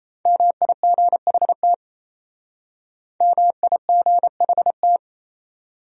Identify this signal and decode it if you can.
Morse code, heard as a keyed tone, about 37 wpm, 700 Hz. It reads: MIGHT MIGHT